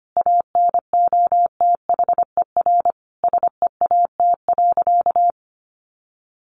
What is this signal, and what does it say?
Morse code, heard as a keyed tone, about 25 wpm, 700 Hz. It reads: ANOTHER SEAT.
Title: Sentences from Top 1500 Words 25wpm